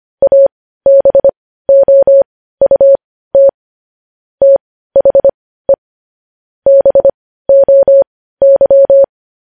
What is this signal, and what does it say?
Morse code, heard as a keyed tone, about 25 wpm, spaced as for 15 wpm, 550 Hz. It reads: ABOUT THE BOY